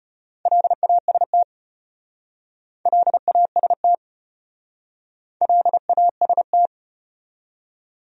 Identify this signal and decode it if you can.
Morse code, heard as a keyed tone, about 33 wpm, 700 Hz. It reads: LAST LAST LAST